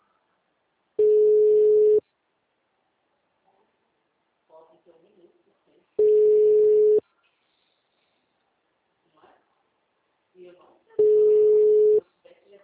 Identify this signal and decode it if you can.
Ringback tone (Europe, 425 Hz), with speech.
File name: out- -2038- - - .2950